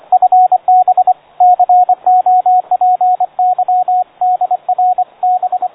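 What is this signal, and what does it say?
Morse code, heard as a keyed tone, about 24 wpm, 730 Hz, with radio noise. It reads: FB COPY DR B
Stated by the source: Morse received on the amateur bands with a narrow CW filter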